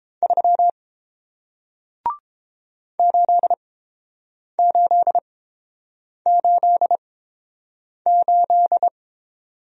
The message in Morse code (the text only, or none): 3 8888